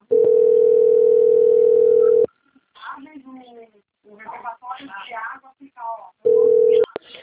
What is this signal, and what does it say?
Ringback tone (North America), 440 + 480 Hz, with speech.
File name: out- - - - - .51580